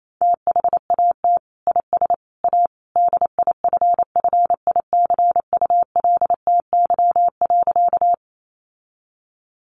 Morse code, heard as a keyed tone, about 28 wpm, 700 Hz. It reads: THAT IS A DIFFICULTY.
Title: Sentences from Top 1500 Words 28wpm